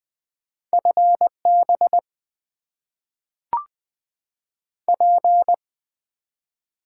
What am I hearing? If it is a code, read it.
Morse code, heard as a keyed tone, about 20 wpm, 700 Hz. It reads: FB P
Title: Top 90 QSO Elements - 8x Character Spacing - 20wpm